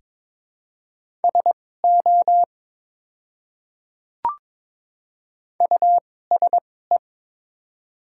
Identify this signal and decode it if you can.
Morse code, heard as a keyed tone, about 22 wpm, spaced as for 16 wpm, 700 Hz. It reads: SO USE
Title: 2x Character Spacing - Top 100 Words - 22wpm